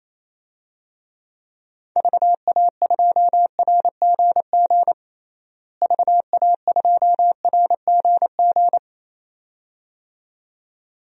Morse code, heard as a keyed tone, about 28 wpm, 700 Hz. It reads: VA2RGG VA2RGG
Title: DXCC-100 Call Signs - 6 Characters - Repeated 2x - 28wpm